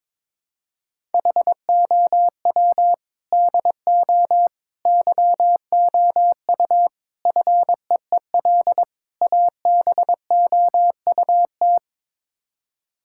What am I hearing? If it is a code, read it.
Morse code, heard as a keyed tone, about 22 wpm, 700 Hz. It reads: HOW DO YOU FEEL ABOUT
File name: practice / 22wpm / Build A Phrase - 5-Word Phrases - 22wpm